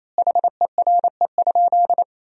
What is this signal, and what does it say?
Morse code, heard as a keyed tone, about 28 wpm, 700 Hz. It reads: HERE?
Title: 2x Word-Spacing - Sentences from Top 300 Words - 28wpm